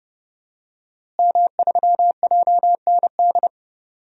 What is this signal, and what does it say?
Morse code, heard as a keyed tone, about 29 wpm, 700 Hz. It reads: M3JND
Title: DXCC-100 Call Signs - 5 Characters - Repeated 3x - 30wpm